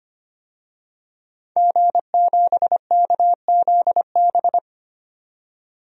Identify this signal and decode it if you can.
Morse code, heard as a keyed tone, about 25 wpm, 700 Hz. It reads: G7KZB
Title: DXCC-100 Call Signs - 5 Characters - 25wpm